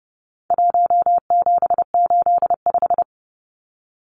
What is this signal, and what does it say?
Morse code, heard as a keyed tone, about 30 wpm, 700 Hz. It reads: 1785
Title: Numbers - 4-Digit 30wpm